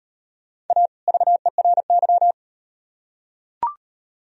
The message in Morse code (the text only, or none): A VERY